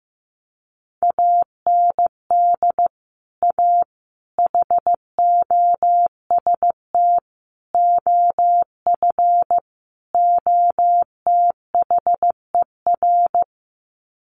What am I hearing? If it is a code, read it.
Morse code, heard as a keyed tone, about 15 wpm, 700 Hz. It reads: AND A HOST OF OTHER
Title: Sets of 5 Words Encore2 15wpm